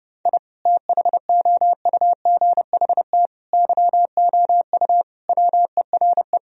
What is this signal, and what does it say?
Morse code, heard as a keyed tone, about 30 wpm, 700 Hz. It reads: I THOUGHT YOU WERE
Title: Sentences from Top 1000 Words - Review Part 4 - 30wpm